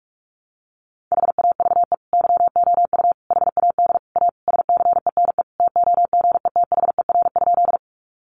Morse code, heard as a keyed tone, about 44 wpm, 700 Hz. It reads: HAVE YOU HAD A SCENE TOGETHER?
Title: Sentences from Top 1100 Words 45wpm